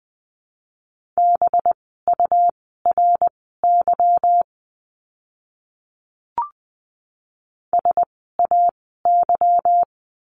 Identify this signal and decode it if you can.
Morse code, heard as a keyed tone, about 20 wpm, spaced as for 14 wpm, 700 Hz. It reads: BURY SAY
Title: 185 Lesson 30f - Letter b - Words 20wpm